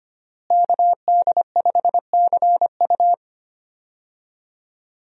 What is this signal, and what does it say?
Morse code, heard as a keyed tone, about 25 wpm, 700 Hz. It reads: KD5CU
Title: USA Call Signs - 5 Characters - Repeated 3x - 25wpm